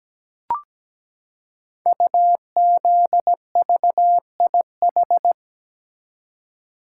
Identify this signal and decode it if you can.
Morse code, heard as a keyed tone, about 17 wpm, 700 Hz. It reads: UZVIH
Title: Code Groups - 5 Letters - 17wpm